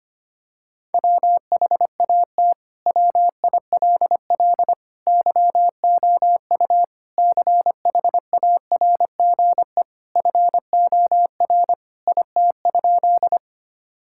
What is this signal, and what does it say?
Morse code, heard as a keyed tone, about 25 wpm, 700 Hz. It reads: WHAT WILL YOU CHARGE FOR IT?
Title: Sentences from Top 1000 Words - Review Part 6 - 25wpm